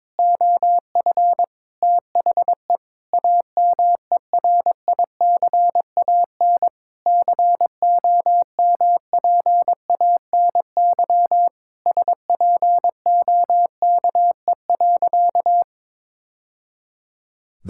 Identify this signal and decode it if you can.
Morse code, heard as a keyed tone, about 22 wpm, 700 Hz. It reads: OF THE AMERICAN COMPANY SPOKE.
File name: practice / 22wpm / Sentences with 3-Word Phrases - 22wpm